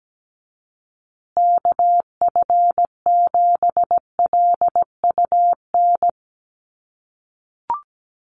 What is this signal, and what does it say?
Morse code, heard as a keyed tone, about 17 wpm, 700 Hz. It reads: KF7LUN
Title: USA Call Signs Repeated 3x 17wpm